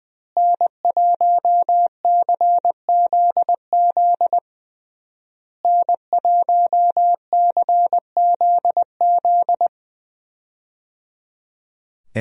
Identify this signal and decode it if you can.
Morse code, heard as a keyed tone, about 20 wpm, 700 Hz. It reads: N1CZZ N1CZZ
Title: USA Call Signs - 5 Characters - Repeated 3x - 20wpm